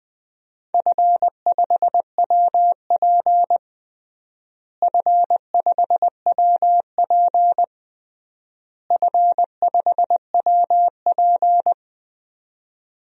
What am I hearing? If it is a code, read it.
Morse code, heard as a keyed tone, about 20 wpm, 700 Hz. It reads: F5WP F5WP F5WP
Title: DXCC-100 Call Signs - 4 Characters - Repeated 3x - 20wpm